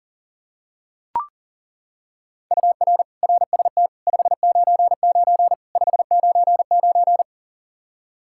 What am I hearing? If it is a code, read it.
Morse code, heard as a keyed tone, about 40 wpm, 700 Hz. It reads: UR RST 599 599
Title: K1USN SST QSO Practice 40wpm